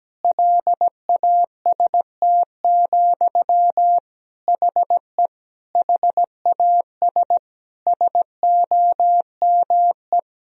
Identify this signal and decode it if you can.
Morse code, heard as a keyed tone, about 17 wpm, 700 Hz. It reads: LAST, HE HAS SOME
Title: Sentences from Top 1000 Words - Review Part 3 - 17wpm